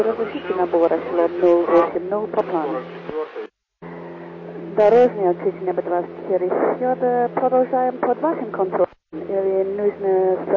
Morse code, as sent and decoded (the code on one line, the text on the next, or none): none